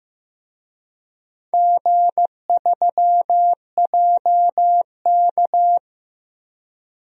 Morse code, heard as a keyed tone, about 15 wpm, 700 Hz. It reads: G3JK
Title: DXCC-100 Call Signs - 4 Characters - Repeated 2x - 15wpm